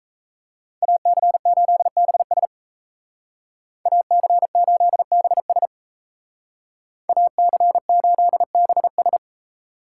Morse code, heard as a keyed tone, about 37 wpm, 700 Hz. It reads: AC8BS AC8BS AC8BS